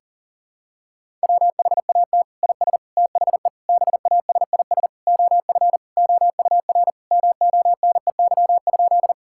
Morse code, heard as a keyed tone, about 37 wpm, 700 Hz. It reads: WHAT IS THE BASIS OF OUR MONEY?
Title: Sentences from Top 1000 Words - Review Part 5 - 40wpm